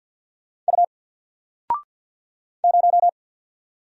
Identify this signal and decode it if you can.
Morse code, heard as a keyed tone, about 50 wpm, 700 Hz. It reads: U 0
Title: Single Letter-Number 50wpm